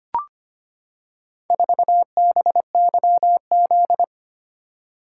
Morse code, heard as a keyed tone, about 25 wpm, 700 Hz. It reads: 4BYZ
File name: practice / 25wpm / Four Letter-Number Combinations - 25wpm